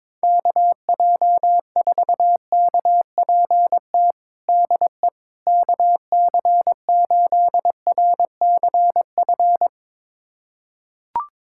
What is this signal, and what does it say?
Morse code, heard as a keyed tone, about 22 wpm, 700 Hz. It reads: KJ4KPT DE KC8RCF
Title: USA Callsign DE Callsign - 6 Characters - 22wpm